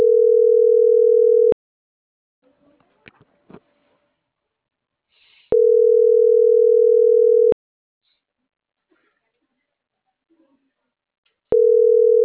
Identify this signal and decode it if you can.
Ringback tone (North America), 440 + 480 Hz.